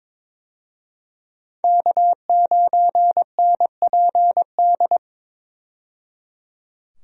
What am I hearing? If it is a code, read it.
Morse code, heard as a keyed tone, about 22 wpm, 700 Hz. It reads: K9NPD
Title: USA Call Signs - 5 Characters - 22wpm